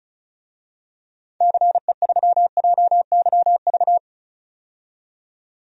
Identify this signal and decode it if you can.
Morse code, heard as a keyed tone, about 36 wpm, 700 Hz. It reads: CE3JYV
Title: DXCC-100 Call Signs - 6 Characters - 35wpm